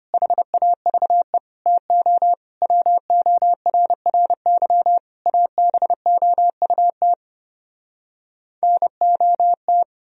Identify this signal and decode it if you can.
Morse code, heard as a keyed tone, about 33 wpm, 700 Hz. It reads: HAVE TO WORRY ABOUT NOT